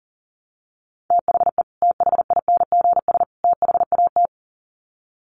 Morse code, heard as a keyed tone, about 40 wpm, 700 Hz. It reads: THE THINGS THAT
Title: Sets of 3 Words 40wpm